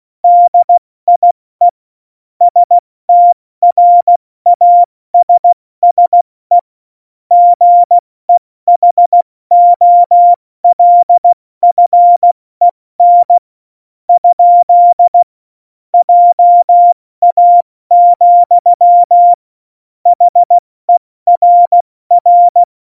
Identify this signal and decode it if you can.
Morse code, heard as a keyed tone, about 16 wpm, spaced as for 14 wpm, 700 Hz. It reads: DIE STRASSE GEHOLFEN? JA, HERR
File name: Morsekurs / Witze / Witz 30. April 2019, 70 F80